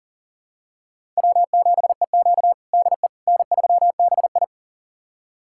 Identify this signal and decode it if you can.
Morse code, heard as a keyed tone, about 40 wpm, 700 Hz. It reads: W7EQ DE N3BI
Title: USA Callsign DE Callsign - 4 Characters - 40wpm